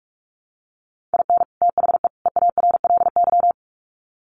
Morse code, heard as a keyed tone, about 44 wpm, 700 Hz. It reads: IN THE EARLY
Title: Sets of 3 Words 45wpm